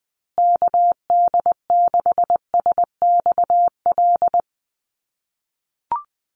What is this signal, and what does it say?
Morse code, heard as a keyed tone, about 19 wpm, 700 Hz. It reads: KD6SXL
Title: USA Call Signs Repeated 2x 20wpm